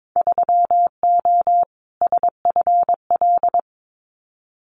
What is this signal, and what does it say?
Morse code, heard as a keyed tone, about 22 wpm, 700 Hz. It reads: 3O SFL
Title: Winter Field Day Exchanges 22wpm